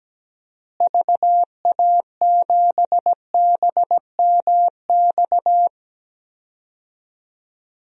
Morse code, heard as a keyed tone, about 17 wpm, 700 Hz. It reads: VA7BMX
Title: DXCC-100 Call Signs - 6 Characters - Repeated 3x - 17wpm